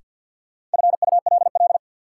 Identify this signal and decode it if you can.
Morse code, heard as a keyed tone, about 51 wpm, 700 Hz. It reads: FULL
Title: Sentences from Top 500 Words - New Word Drill - 50wpm